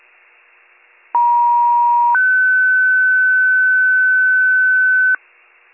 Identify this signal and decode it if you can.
Two-tone page, heard as a one-second tone, then a three-second tone, about 950 then 1500 Hz, with noise. It is heard from 1.1 to 5.2 s.